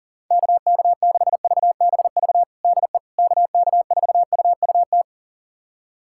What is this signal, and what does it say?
Morse code, heard as a keyed tone, about 40 wpm, 700 Hz. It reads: KK6VBV DE KK4UUT
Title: USA Callsign DE Callsign - 6 Characters - 40wpm